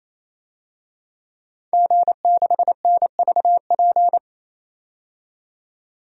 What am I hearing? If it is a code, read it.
Morse code, heard as a keyed tone, about 28 wpm, 700 Hz. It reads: G6NVP
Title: DXCC-100 Call Signs - 5 Characters - 28wpm